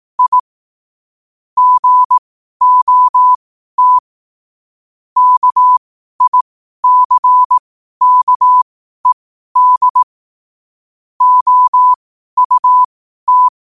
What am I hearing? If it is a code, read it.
Morse code, heard as a keyed tone, about 18 wpm, spaced as for 12 wpm, 1000 Hz. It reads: I GOT KICKED OUT
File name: qotdfiles / 12wpm